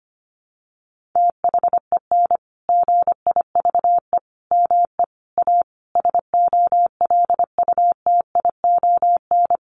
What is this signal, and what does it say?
Morse code, heard as a keyed tone, about 25 wpm, 700 Hz. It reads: THEN GIVE ME A SOLUTION
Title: Sentences from Top 1300 Words 25wpm